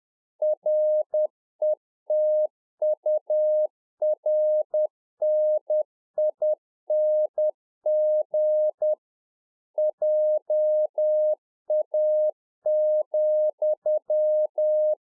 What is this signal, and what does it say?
Morse code, heard as a keyed tone, about 10 wpm, 610 Hz. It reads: RETURNING JA,